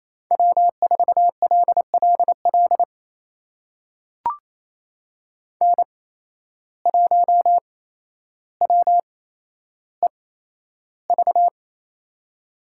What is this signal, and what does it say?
Morse code, heard as a keyed tone, about 28 wpm, spaced as for 12 wpm, 700 Hz. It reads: W4LLL N1WEV
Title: USA Call Signs - 5 Characters - 8x Character Spacing - 28wpm